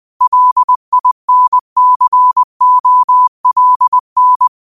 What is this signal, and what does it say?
Morse code, heard as a keyed tone, about 20 wpm, 1000 Hz. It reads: LINCOLN
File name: qotdfiles / 20wpm